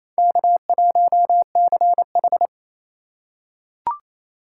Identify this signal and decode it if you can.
Morse code, heard as a keyed tone, about 28 wpm, 700 Hz. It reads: K1CH
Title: USA Call Signs - 4 Characters - 2x Character Spacing - 28wpm